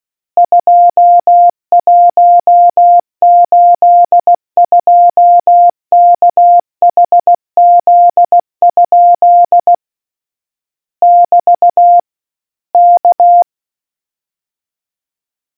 Morse code, heard as a keyed tone, about 16 wpm, 700 Hz. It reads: 2182KHZ? <BT> K